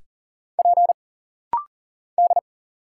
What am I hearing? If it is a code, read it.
Morse code, heard as a keyed tone, about 39 wpm, 700 Hz. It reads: P D
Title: Single Letters - Rapid-Fire 40wpm